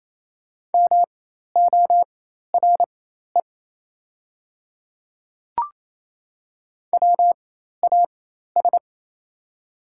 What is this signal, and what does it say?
Morse code, heard as a keyed tone, about 28 wpm, spaced as for 13 wpm, 700 Hz. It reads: MORE WAS
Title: 4x Character Spacing - Top 300 Words - 28wpm